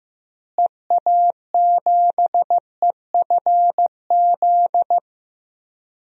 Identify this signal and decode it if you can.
Morse code, heard as a keyed tone, about 15 wpm, 700 Hz. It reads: EA7EFZ
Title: DXCC-100 Call Signs - 6 Characters - Repeated 2x - 15wpm